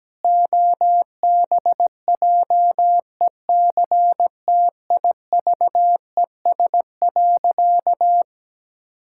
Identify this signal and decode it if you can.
Morse code, heard as a keyed tone, about 17 wpm, 700 Hz. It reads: OBJECTIVES.
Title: Pangram Challenge v2 - 17wpm